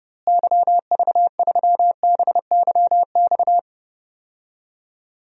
Morse code, heard as a keyed tone, about 30 wpm, 700 Hz. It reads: YV3BYX
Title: DXCC-100 Call Signs - 6 Characters - Repeated 2x - 30wpm